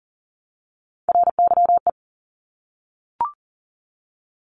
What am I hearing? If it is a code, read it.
Morse code, heard as a keyed tone, about 42 wpm, 700 Hz. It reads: RYE
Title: Multi-letter Phonemes Repeated 3x 40wpm